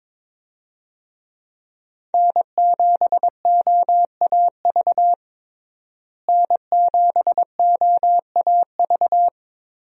Morse code, heard as a keyed tone, about 21 wpm, 700 Hz. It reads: N7OAV N7OAV
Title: USA Call Signs - 5 Characters - Repeated 2x - 22wpm